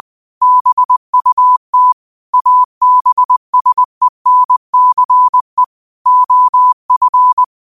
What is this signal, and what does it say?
Morse code, heard as a keyed tone, about 20 wpm, 1000 Hz. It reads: BUT ABSENCE OF
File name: qotdfiles / 20wpm